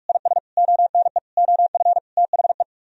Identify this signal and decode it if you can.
Morse code, heard as a keyed tone, about 49 wpm, 700 Hz. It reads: IS ONE OF THE